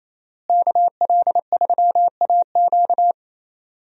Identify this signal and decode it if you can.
Morse code, heard as a keyed tone, about 28 wpm, 700 Hz. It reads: KL3AQ
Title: USA Call Signs - 5 Characters - Repeated 3x - 28wpm